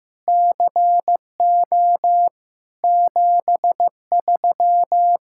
Morse code, heard as a keyed tone, about 15 wpm, 700 Hz. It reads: CO 73
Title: Speed-Racing Common QSO Phrases Encore - 15wpm